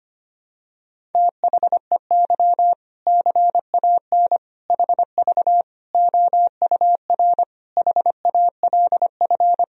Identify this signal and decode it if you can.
Morse code, heard as a keyed tone, about 25 wpm, 700 Hz. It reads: THEY CAN HV OUR HALF
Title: Sentences from Top 400 Words and Abbreviations 25wpm